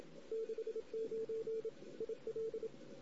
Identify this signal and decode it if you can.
Morse code, heard as a keyed tone, about 30 wpm, 440 Hz. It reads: 69 IL
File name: CW / 2008-ss-cw-k5tr-op-n5rz -16:00